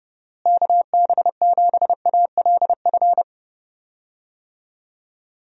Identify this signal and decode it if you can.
Morse code, heard as a keyed tone, about 30 wpm, 700 Hz. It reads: KB7ALF
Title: USA Call Signs - 6 Characters - Repeated 3x - 30wpm